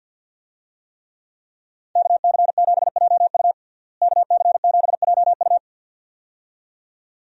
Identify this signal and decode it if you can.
Morse code, heard as a keyed tone, about 49 wpm, 700 Hz. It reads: KC7JU KC7JU
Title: USA Call Signs - 5 Characters - Repeated 2x - 50wpm